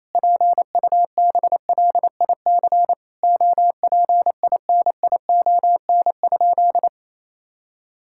Morse code, heard as a keyed tone, about 28 wpm, 700 Hz. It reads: PUBLIC OPINION?
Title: Sentences from Top 1600 Words 28wpm